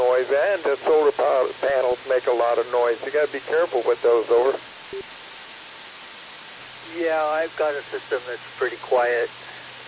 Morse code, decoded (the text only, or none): none